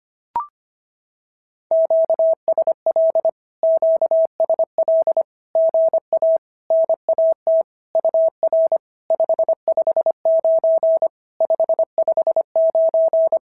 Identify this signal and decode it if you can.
Morse code, heard as a keyed tone, about 25 wpm, 650 Hz. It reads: QSL QSL GA NAT UR 559 559
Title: SOTA QSO Practice 25wpm